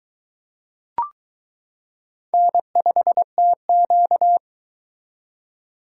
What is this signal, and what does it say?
Morse code, heard as a keyed tone, about 20 wpm, 700 Hz. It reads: N5TQ